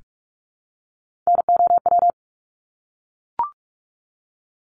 Morse code, heard as a keyed tone, about 44 wpm, 700 Hz. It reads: NOW
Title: Top 200 Words 45wpm